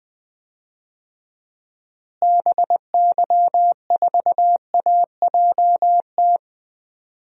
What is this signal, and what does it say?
Morse code, heard as a keyed tone, about 20 wpm, 700 Hz. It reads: BY4AJT
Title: DXCC-100 Call Signs - 6 Characters - Repeated 2x - 20wpm